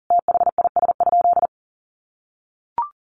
Morse code, heard as a keyed tone, about 40 wpm, 700 Hz. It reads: THIS?
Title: Sentences from Top 100 Words 40wpm